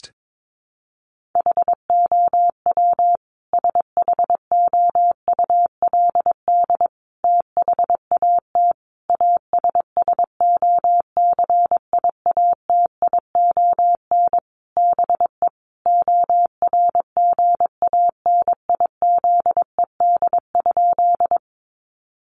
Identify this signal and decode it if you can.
Morse code, heard as a keyed tone, about 22 wpm, 700 Hz. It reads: HOW SHOULD THAT ASSOCIATION BE ORGANIZED?
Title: Sentences from Top 1300 Words 22wpm